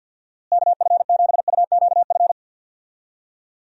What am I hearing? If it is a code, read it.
Morse code, heard as a keyed tone, about 51 wpm, 700 Hz. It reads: KF7UQF
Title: USA Call Signs - 6 Characters - Repeated 2x - 50wpm